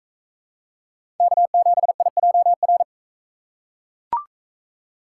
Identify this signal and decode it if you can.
Morse code, heard as a keyed tone, about 40 wpm, 700 Hz. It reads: K7IJR